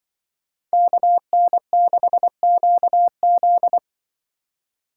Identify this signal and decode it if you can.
Morse code, heard as a keyed tone, about 25 wpm, 700 Hz. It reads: KN6QZ
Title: Speed Racing - USA 2x2 Call Signs 20wpm